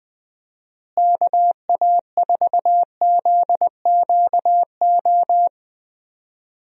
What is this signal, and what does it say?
Morse code, heard as a keyed tone, about 20 wpm, 700 Hz. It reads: KA4ZQO
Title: USA Call Signs - 6 Characters - Repeated 3x - 20wpm